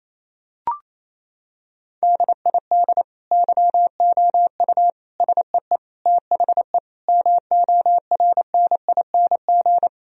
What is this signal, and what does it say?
Morse code, heard as a keyed tone, about 28 wpm, 700 Hz. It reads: DID YOU SEE THE MORNING
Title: Sentences from Top 1000 Words - Review Part 2 - 28wpm